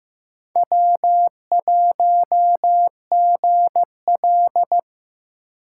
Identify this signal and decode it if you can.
Morse code, heard as a keyed tone, about 15 wpm, 700 Hz. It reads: W1GL
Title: USA Call Signs - 4 Characters - 4x Character Spacing - 15wpm